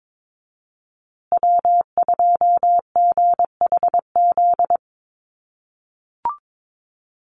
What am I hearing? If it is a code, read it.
Morse code, heard as a keyed tone, about 22 wpm, 700 Hz. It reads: W2GHZ
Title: USA Call Signs Repeated 2x 22wpm